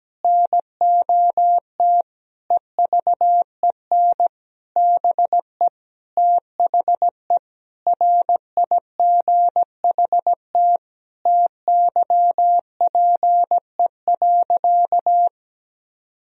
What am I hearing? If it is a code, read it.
Morse code, heard as a keyed tone, about 17 wpm, 700 Hz. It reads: NOT EVEN BE THE RIGHT TYPE.
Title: Sentences from Top 1000 Words - Review Part 3 - 17wpm